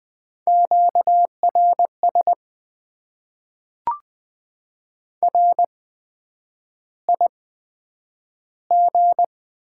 Morse code, heard as a keyed tone, about 20 wpm, spaced as for 9 wpm, 700 Hz. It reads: QRS RIG
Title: Top 90 QSO Elements - 8x Character Spacing - 20wpm